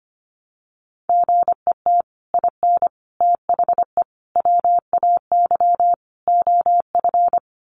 Morse code, heard as a keyed tone, about 25 wpm, 700 Hz. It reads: GET IN THE WAY OF
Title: Sets of 5 Words Encore 25wpm